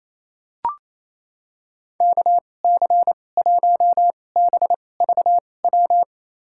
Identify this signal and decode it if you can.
Morse code, heard as a keyed tone, about 28 wpm, spaced as for 20 wpm, 700 Hz. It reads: KC1BVW
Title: USA Call Signs - 6 Characters - 2x Character Spacing - 28wpm